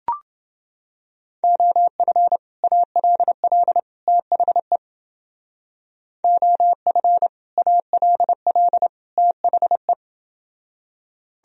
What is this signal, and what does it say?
Morse code, heard as a keyed tone, about 29 wpm, 700 Hz. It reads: OF ALL THE OF ALL THE